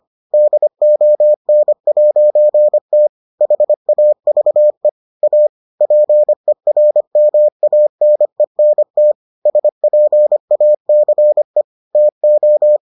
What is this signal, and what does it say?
Morse code, heard as a keyed tone, about 25 wpm, 600 Hz. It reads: DON'T HAVE A PERMANENT SPACE TO